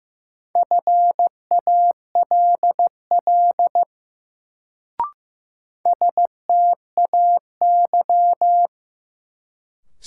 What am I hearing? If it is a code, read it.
Morse code, heard as a keyed tone, about 15 wpm, 700 Hz. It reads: FALL STAY
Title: Sentences from Top 500 Words - New Word Drill - 15wpm